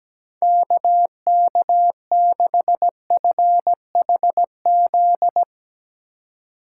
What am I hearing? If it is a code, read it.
Morse code, heard as a keyed tone, about 17 wpm, 700 Hz. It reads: KK6FHZ
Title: Speed Racing - USA 2x3 Call Signs 17wpm